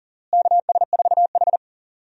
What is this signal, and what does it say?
Morse code, heard as a keyed tone, about 40 wpm, 700 Hz. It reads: KS4H
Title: USA Call Signs - 4 Characters - Repeated 3x - 40wpm